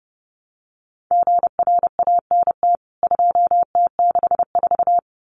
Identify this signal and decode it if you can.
Morse code, heard as a keyed tone, about 30 wpm, 700 Hz. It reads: GRANT 2T64
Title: CWT Exchange 30wpm